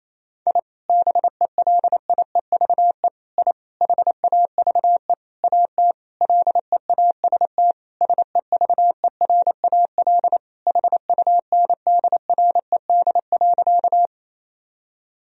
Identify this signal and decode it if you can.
Morse code, heard as a keyed tone, about 28 wpm, 700 Hz. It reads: I BELIEVE I HAVE AT LEAST SEVERAL HUNDRED.
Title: Sentences from Top 1000 Words - Review Part 2 - 28wpm